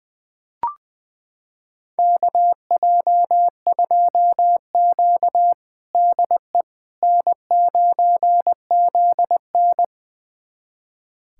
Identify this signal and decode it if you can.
Morse code, heard as a keyed tone, about 21 wpm, 700 Hz. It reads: KJ2Q DE N9ZN
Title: USA Callsign DE Callsign - 4 Characters - 20wpm